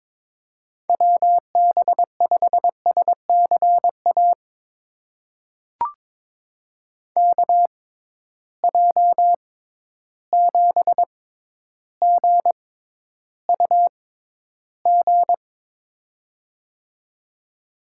Morse code, heard as a keyed tone, about 22 wpm, spaced as for 11 wpm, 700 Hz. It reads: WB5SCA KJ7GUG
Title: USA Call Signs - 6 Characters - 6x Character Spacing - 22wpm